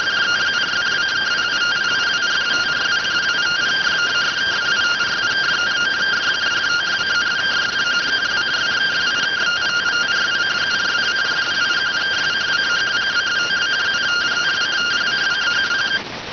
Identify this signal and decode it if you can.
Two-tone teleprinter signal, about 170 Hz shift.